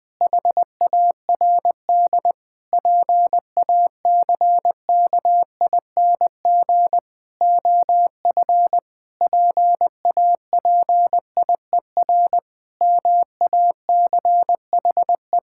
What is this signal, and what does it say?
Morse code, heard as a keyed tone, about 21 wpm, 700 Hz. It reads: HARD PACKING OF PAPIER MACHE
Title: Pangram Challenge v2 - 20wpm